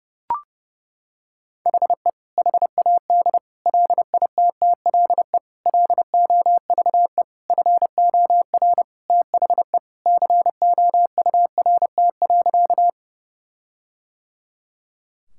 Morse code, heard as a keyed tone, about 30 wpm, 700 Hz. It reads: HE HAD LITTLE LOVE FOR THE COURT.
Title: Sentences from Top 1600 Words 30wpm